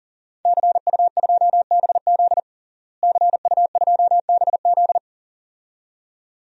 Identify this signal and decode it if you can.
Morse code, heard as a keyed tone, about 40 wpm, 700 Hz. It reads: CU2BZ CU2BZ
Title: DXCC-100 Call Signs - 5 Characters - Repeated 2x - 40wpm